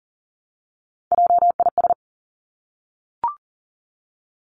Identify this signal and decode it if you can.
Morse code, heard as a keyed tone, about 41 wpm, 700 Hz. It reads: JIS
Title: Three Challenging Letter-Number Combinations 40wpm